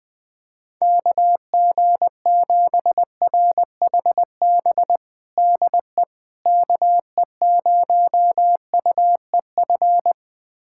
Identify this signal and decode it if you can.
Morse code, heard as a keyed tone, about 21 wpm, 700 Hz. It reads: KG7RHB DE KE0UEF
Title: USA Callsign DE Callsign - 6 Characters - 20wpm